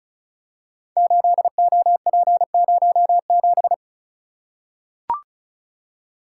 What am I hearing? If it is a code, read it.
Morse code, heard as a keyed tone, about 36 wpm, 700 Hz. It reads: 8OP07
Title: Code Groups - 5 Letter:Numbers - 35wpm